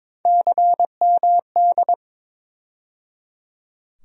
Morse code, heard as a keyed tone, about 22 wpm, 700 Hz. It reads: CMD
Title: USA Call Sign Postfixes - 3 Letters 22wpm